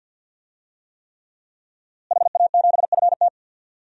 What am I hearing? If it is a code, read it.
Morse code, heard as a keyed tone, about 52 wpm, 700 Hz. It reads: HA7FT
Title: DXCC-100 Call Signs - 5 Characters - Repeated 3x - 50wpm